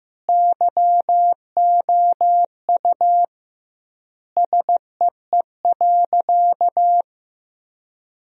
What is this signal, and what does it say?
Morse code, heard as a keyed tone, about 16 wpm, 700 Hz. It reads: YOU SEE.
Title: 2x Word-Spacing - Sentences from Top 100 Words - 15wpm